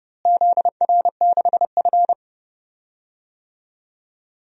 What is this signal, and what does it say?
Morse code, heard as a keyed tone, about 30 wpm, 700 Hz. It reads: ZR6F
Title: DXCC-100 Call Signs - 4 Characters - Repeated 3x - 30wpm